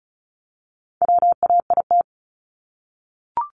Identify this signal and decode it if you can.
Morse code, heard as a keyed tone, about 34 wpm, 700 Hz. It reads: WAIT
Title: Top 500 Words Repeated 35wpm